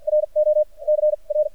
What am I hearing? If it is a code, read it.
Morse code, heard as a keyed tone, about 50 wpm, spaced as for 41 wpm, 610 Hz. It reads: RO C R
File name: archived / CRSnet